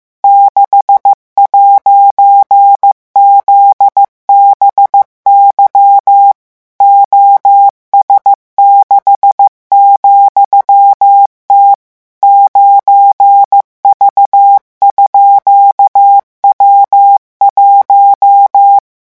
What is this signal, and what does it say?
Morse code, heard as a keyed tone, about 15 wpm, 800 Hz. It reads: 6'ZBY OS6,T 9V_W1